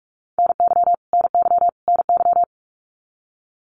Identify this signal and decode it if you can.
Morse code, heard as a keyed tone, about 44 wpm, 700 Hz. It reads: NY NY NY
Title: US States Repeated 3x 45wpm